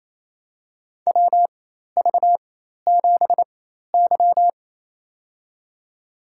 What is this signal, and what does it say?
Morse code, heard as a keyed tone, about 28 wpm, spaced as for 13 wpm, 700 Hz. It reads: WV7Y
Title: USA Call Signs - 4 Characters - 4x Character Spacing - 28wpm